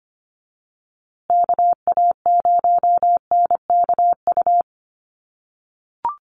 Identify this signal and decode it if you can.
Morse code, heard as a keyed tone, about 25 wpm, 700 Hz. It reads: KA0NKU
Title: USA Call Signs Repeated 3x 25wpm